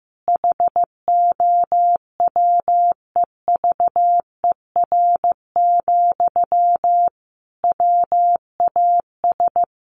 Morse code, heard as a keyed tone, about 15 wpm, 700 Hz. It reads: HOWEVER, WAS
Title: Sentences from Top 1400 Words 15wpm